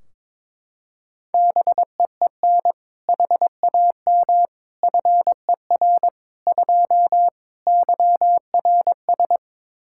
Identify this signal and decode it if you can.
Morse code, heard as a keyed tone, about 22 wpm, 700 Hz. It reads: BEEN HAM FER 2 YRS
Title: K1USN SST QSO Practice 22wpm